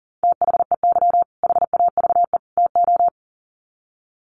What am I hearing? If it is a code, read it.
Morse code, heard as a keyed tone, about 37 wpm, 700 Hz. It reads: THEY HAVE TO
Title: Sets of 3 Words 40wpm